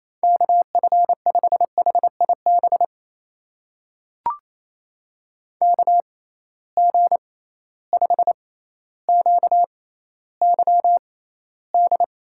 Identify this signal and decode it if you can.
Morse code, heard as a keyed tone, about 28 wpm, spaced as for 14 wpm, 700 Hz. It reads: KF5HIB KG5QYD
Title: USA Call Signs - 6 Characters - 6x Character Spacing - 28wpm